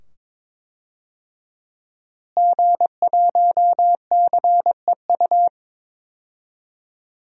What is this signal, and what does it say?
Morse code, heard as a keyed tone, about 22 wpm, 700 Hz. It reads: G1CEU
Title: DXCC-100 Call Signs - 5 Characters - Repeated 2x - 22wpm